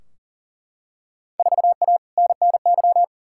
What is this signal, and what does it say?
Morse code, heard as a keyed tone, about 40 wpm, 700 Hz. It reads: 4A NNY
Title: Field Day Exchanges - 40wpm